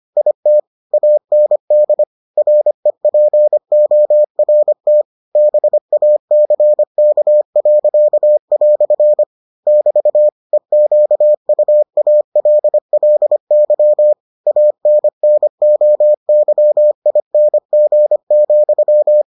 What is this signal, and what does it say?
Morse code, heard as a keyed tone, about 25 wpm, 600 Hz. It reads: IT AND REPORT BACK." <BT> EQUALLY ANNOYING,